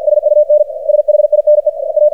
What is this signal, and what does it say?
Morse code, heard as a keyed tone, about 47 wpm, 600 Hz, with radio noise. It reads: SUNSHINE A
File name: archived / CRSnet